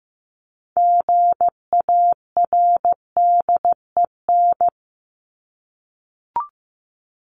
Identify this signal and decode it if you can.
Morse code, heard as a keyed tone, about 15 wpm, 700 Hz. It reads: GARDEN
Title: Top 501-750 Words 15wpm